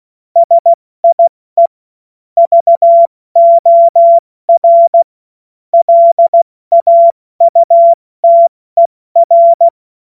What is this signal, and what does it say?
Morse code, heard as a keyed tone, about 16 wpm, 680 Hz. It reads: SIE VOR LAUTER
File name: Morsekurs / Witze / Witz 02. August 2022 70 F80